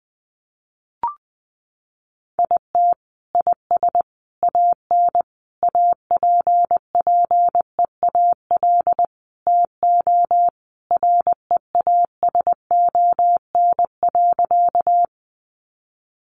Morse code, heard as a keyed tone, about 20 wpm, 700 Hz. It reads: IT IS AN APPEAL TO REASON.
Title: Sentences from Top 1300 Words 20wpm